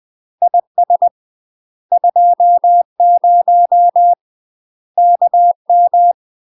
Morse code, heard as a keyed tone, about 20 wpm, 700 Hz. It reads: IS 20 KM
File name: UK MASTER QSOs TRAINING/20WPM